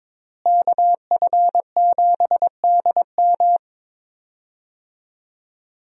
Morse code, heard as a keyed tone, about 21 wpm, 700 Hz. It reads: KF7DM
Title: USA Call Signs - 5 Characters - Repeated 3x - 22wpm